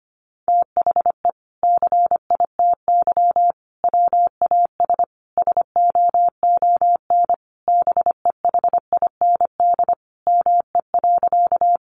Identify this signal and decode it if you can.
Morse code, heard as a keyed tone, about 25 wpm, 700 Hz. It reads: THE CITY WAS SOON BEHIND ME.
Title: Sentences from Top 500 Words 25wpm